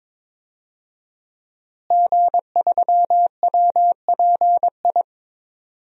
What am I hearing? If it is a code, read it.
Morse code, heard as a keyed tone, about 22 wpm, 700 Hz. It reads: G3WPI
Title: DXCC-100 Call Signs - 5 Characters - Repeated 2x - 22wpm